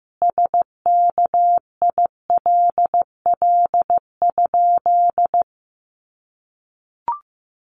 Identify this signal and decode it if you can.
Morse code, heard as a keyed tone, about 15 wpm, 700 Hz. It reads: SKILL?
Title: Sentences from Top 1200 Words 15wpm